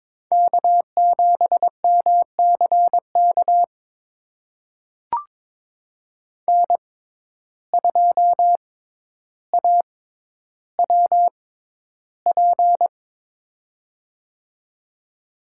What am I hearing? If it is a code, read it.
Morse code, heard as a keyed tone, about 22 wpm, spaced as for 11 wpm, 700 Hz. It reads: K7MCK N2AWP
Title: USA Call Signs - 5 Characters - 6x Character Spacing - 22wpm